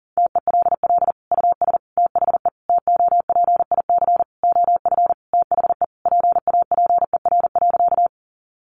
Morse code, heard as a keyed tone, about 40 wpm, 700 Hz. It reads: TELL US THE TOPIC OF THE PAPER.
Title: Sentences from Top 1200 Words 40wpm